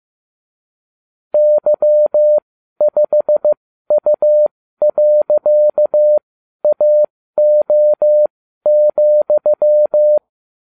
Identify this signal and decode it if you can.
Morse code, heard as a keyed tone, about 15 wpm, spaced as for 12 wpm, 600 Hz. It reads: Y5U.AO,